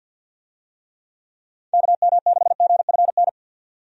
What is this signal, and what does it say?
Morse code, heard as a keyed tone, about 51 wpm, 700 Hz. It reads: KM6GFN
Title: USA Call Signs - 6 Characters - 50wpm